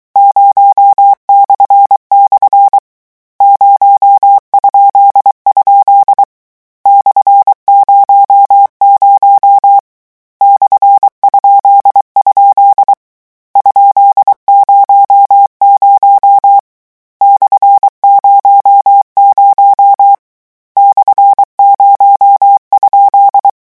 Morse code, heard as a keyed tone, about 23 wpm, 800 Hz. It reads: 0// 0?? /00 /?? ?00 /00 /0?
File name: morse mp3 / lecon13-vitesse 22